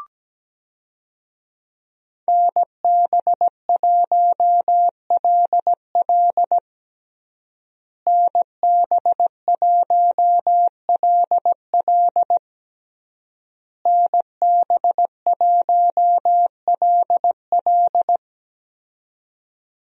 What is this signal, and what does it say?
Morse code, heard as a keyed tone, about 17 wpm, 700 Hz. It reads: NB1LL NB1LL NB1LL
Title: USA Call Signs - 5 Characters - Repeated 3x - 17wpm